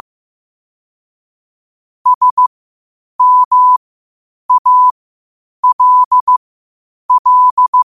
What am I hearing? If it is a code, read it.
Morse code, heard as a keyed tone, about 15 wpm, spaced as for 8 wpm, 1000 Hz. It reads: SMALL
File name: qotdfiles / 8wpm